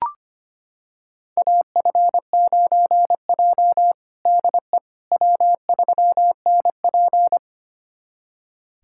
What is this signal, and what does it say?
Morse code, heard as a keyed tone, about 25 wpm, 700 Hz. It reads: AF9J DE W3NP
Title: USA Callsign DE Callsign - 4 Characters - 25wpm